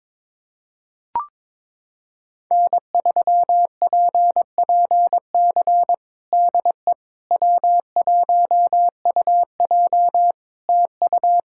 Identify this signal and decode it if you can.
Morse code, heard as a keyed tone, about 22 wpm, 700 Hz. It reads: N3PPC DE W1UJ TU
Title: Practice QSO - Exchange 4 - Ending - 22wpm